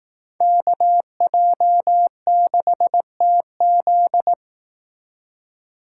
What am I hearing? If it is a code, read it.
Morse code, heard as a keyed tone, about 19 wpm, 700 Hz. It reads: KJ6TZ
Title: Speed Racing - USA 2x2 Call Signs 15wpm